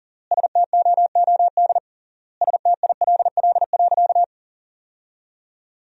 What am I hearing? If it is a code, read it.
Morse code, heard as a keyed tone, about 40 wpm, 700 Hz. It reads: STOOD STILL.
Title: 3x Word-Spacing - Sentences from Top 500 Words - 40wpm